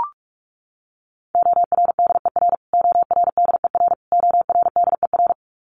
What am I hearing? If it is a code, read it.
Morse code, heard as a keyed tone, about 44 wpm, 700 Hz. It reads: ORDER ORDER ORDER
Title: Top 500 Words Repeated 45wpm